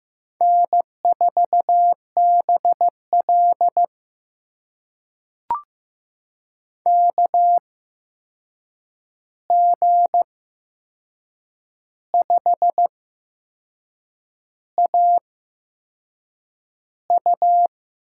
Morse code, heard as a keyed tone, about 15 wpm, spaced as for 6 wpm, 700 Hz. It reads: N4BL KG5AU
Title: USA Call Signs - 8x Character Spacing - 15wpm